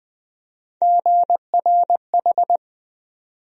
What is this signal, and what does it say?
Morse code, heard as a keyed tone, about 21 wpm, 700 Hz. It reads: GRH
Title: USA Call Sign Postfixes - 3 Letters 20wpm